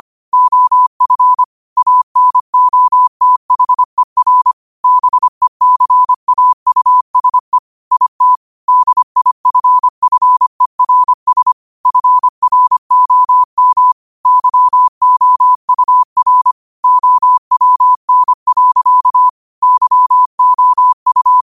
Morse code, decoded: OF ANOTHER BECAUSE IT DIFFERS FROM YOUR OWN. YOU

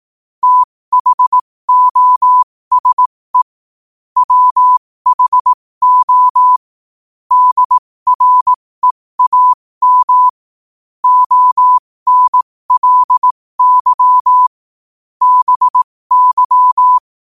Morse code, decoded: THOSE WHO DREAM ONLY BY

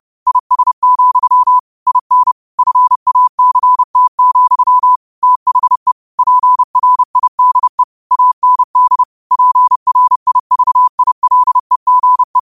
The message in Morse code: IS, IN FACT, THE PRIDE AND PRIVILEGE

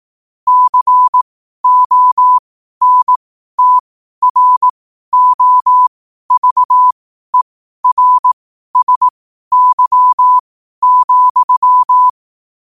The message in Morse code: CONTROVERSY,

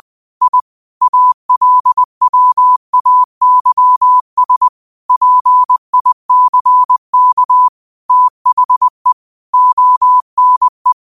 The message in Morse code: I ALWAYS PICK THE ONE

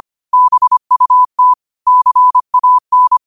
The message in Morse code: BUT CAN

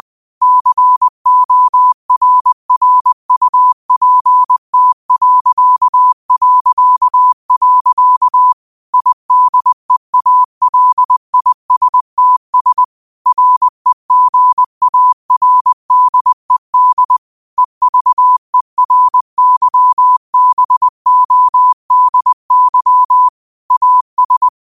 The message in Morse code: CORRUPT... IDEALISTS REGARDED EVERYBODY AS

